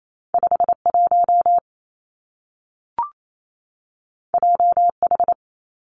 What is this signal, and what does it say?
Morse code, heard as a keyed tone, about 28 wpm, 700 Hz. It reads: 5 1 J H
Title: Two Challenging Letter-Number Combinations 28wpm